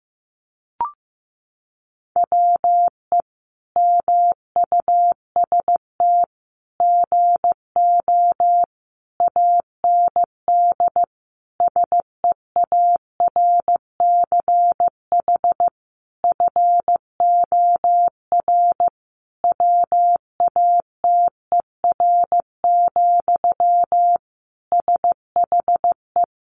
Morse code, heard as a keyed tone, about 15 wpm, 700 Hz. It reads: WE MUST GO AND SEARCH FOR WATER, SHE
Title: Sentences from Top 800 Words 15wpm